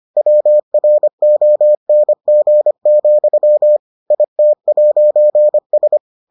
Morse code, heard as a keyed tone, about 25 wpm, 600 Hz. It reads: WRONG, IT'S